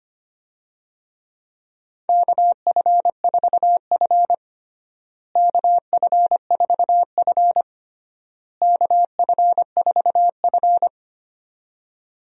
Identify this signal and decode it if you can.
Morse code, heard as a keyed tone, about 25 wpm, 700 Hz. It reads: KF4F KF4F KF4F
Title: USA Call Signs - 4 Characters - Repeated 3x - 25wpm